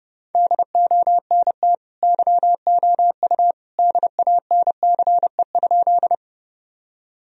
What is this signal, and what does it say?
Morse code, heard as a keyed tone, about 30 wpm, 700 Hz. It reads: DONT YOU DANCE?
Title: Sentences from Top 1000 Words - Review Part 4 - 30wpm